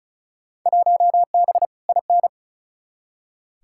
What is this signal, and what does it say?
Morse code, heard as a keyed tone, about 36 wpm, 700 Hz. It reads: 1B IN
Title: Field Day Exchanges - 35wpm